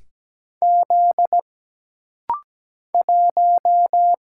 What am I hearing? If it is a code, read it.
Morse code, heard as a keyed tone, about 17 wpm, 700 Hz. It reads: Z 1
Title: Single Letter-Number - Rapid-Fire 17wpm